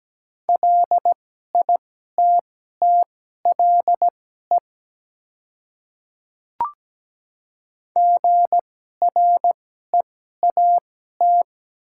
Morse code, heard as a keyed tone, about 17 wpm, spaced as for 12 wpm, 700 Hz. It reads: LITTLE GREAT
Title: 2x Character Spacing - Top 200 Words - 17wpm